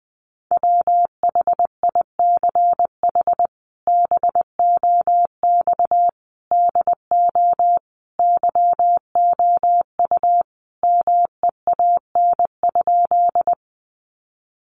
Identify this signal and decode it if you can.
Morse code, heard as a keyed tone, about 20 wpm, 700 Hz. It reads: WHICH BOX DO YOU MEAN?
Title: Sentences from Top 500 Words 20wpm